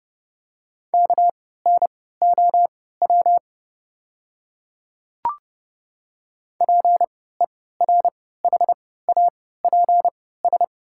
Morse code, heard as a keyed tone, about 30 wpm, spaced as for 17 wpm, 700 Hz. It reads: KNOW PERHAPS
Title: 3x Character Spacing - Top 500 Words - 30wpm